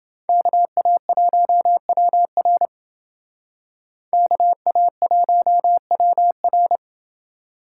Morse code, heard as a keyed tone, about 28 wpm, 700 Hz. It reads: KA1WR KA1WR